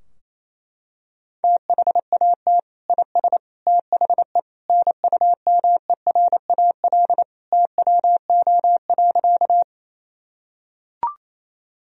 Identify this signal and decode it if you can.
Morse code, heard as a keyed tone, about 28 wpm, 700 Hz. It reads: THAT IS THE NUMERAL TWO.
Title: Sentences from Top 1000 Words - Review Part 2 - 28wpm